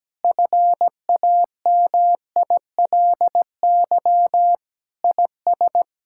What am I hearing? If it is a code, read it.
Morse code, heard as a keyed tone, about 17 wpm, 700 Hz. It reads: FAMILY IS
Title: Sentences from Top 1600 Words 17wpm